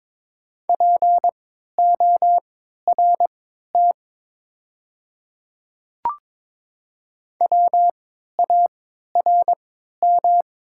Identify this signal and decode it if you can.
Morse code, heard as a keyed tone, about 22 wpm, spaced as for 13 wpm, 700 Hz. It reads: PORT WARM
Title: 3x Character Spacing - Top 500 Words - 22wpm